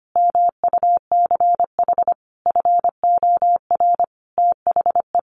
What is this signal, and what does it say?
Morse code, heard as a keyed tone, about 25 wpm, 700 Hz. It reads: MUCH FOR THE
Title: Sentences from Top 1000 Words 25wpm